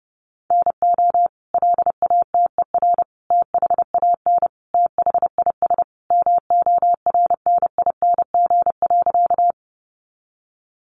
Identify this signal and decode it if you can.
Morse code, heard as a keyed tone, about 30 wpm, 700 Hz. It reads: NO LATER THAN THIS MORNING.
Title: Sentences from Top 700 Words 30wpm